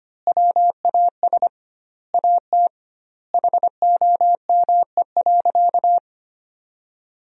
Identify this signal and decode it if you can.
Morse code, heard as a keyed tone, about 25 wpm, 700 Hz. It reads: WAS AT HOME.
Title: 2x Word-Spacing - Sentences from Top 400 Words - 25wpm